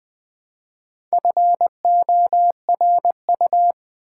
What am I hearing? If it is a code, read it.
Morse code, heard as a keyed tone, about 21 wpm, 700 Hz. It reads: FORU
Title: Build A Word - 1 Letter At A Time - 3 to 5-Letter Words - 20wpm